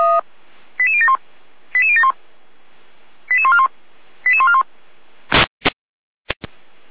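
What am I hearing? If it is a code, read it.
Five-tone selective call (ZVEI), heard as quick runs of five tones, tones of about 70 ms. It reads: 88071 88071 88141 88141
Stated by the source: ZVEI five-tone selective call 88012, received by SDR, recorded by feuerwehr-orlamuende.org.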